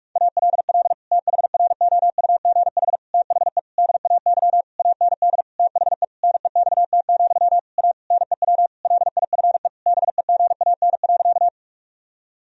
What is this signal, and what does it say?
Morse code, heard as a keyed tone, about 46 wpm, 700 Hz. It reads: ALL THROUGH THE DAY AND THE NEXT, A NEW LIFE BEGAN.
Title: Sentences from Top 1000 Words - Review Part 1 - 45wpm